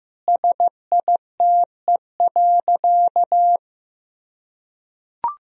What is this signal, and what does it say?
Morse code, heard as a keyed tone, about 15 wpm, 700 Hz. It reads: SITE.
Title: Sentences from Top 1000 Words - Review Part 4 - 15wpm